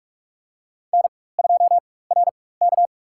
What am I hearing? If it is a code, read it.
Morse code, heard as a keyed tone, about 42 wpm, spaced as for 21 wpm, 700 Hz. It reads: N2RK
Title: USA Call Signs - 4 Characters - 4x Character Spacing - 45wpm